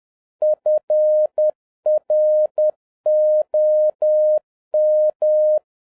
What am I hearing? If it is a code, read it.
Morse code, heard as a keyed tone, about 10 wpm, 610 Hz. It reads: FROM